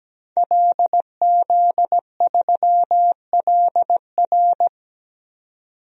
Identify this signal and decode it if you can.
Morse code, heard as a keyed tone, about 17 wpm, 700 Hz. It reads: LZ3LR
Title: DXCC-100 Call Signs - 5 Characters - 17wpm